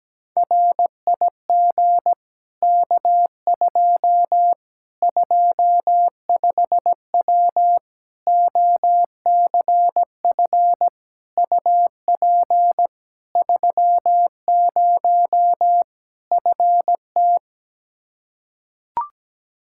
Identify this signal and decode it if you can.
Morse code, heard as a keyed tone, about 17 wpm, 700 Hz. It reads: RIG K2 25W OCF UP 30 FT
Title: Practice QSO - Exchange 2 - Info - 17wpm